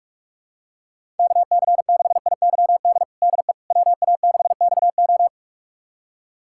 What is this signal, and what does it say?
Morse code, heard as a keyed tone, about 41 wpm, 700 Hz. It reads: KC6IYD DE WA6XO
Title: USA Callsign DE Callsign 45wpm